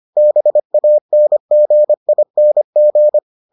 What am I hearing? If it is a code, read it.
Morse code, heard as a keyed tone, about 25 wpm, 600 Hz. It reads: BANGING